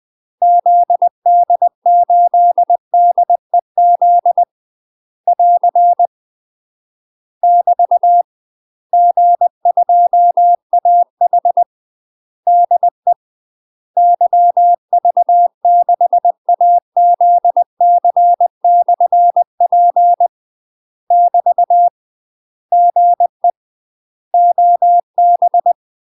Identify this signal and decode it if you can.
Morse code, heard as a keyed tone, about 20 wpm, 700 Hz. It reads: ZD8DEZ <AR> <BT> G2AH DE YV6AZC/P <BT> GE OB
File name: UK MASTER QSOs TRAINING / 20WPM / UK MASTER QSOs 0032